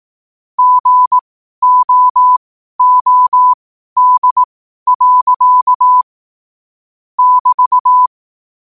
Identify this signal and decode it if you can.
Morse code, heard as a keyed tone, about 18 wpm, spaced as for 12 wpm, 1000 Hz. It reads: GOOD. <BT>
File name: qotdfiles / 12wpm